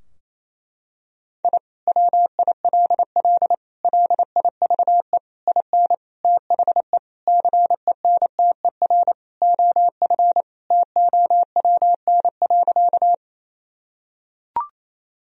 Morse code, heard as a keyed tone, about 28 wpm, 700 Hz. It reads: I WILL LIVE IN THE CENTER OF TOWN.
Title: Sentences from Top 1000 Words - Review Part 2 - 28wpm